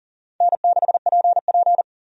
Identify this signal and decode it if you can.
Morse code, heard as a keyed tone, about 37 wpm, 700 Hz. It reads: N6PP
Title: USA Call Signs - 4 Characters - 40wpm